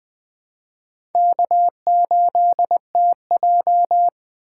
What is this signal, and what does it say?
Morse code, heard as a keyed tone, about 21 wpm, 700 Hz. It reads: K8TJ
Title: Speed Racing - USA 1x2 Call Signs 17wpm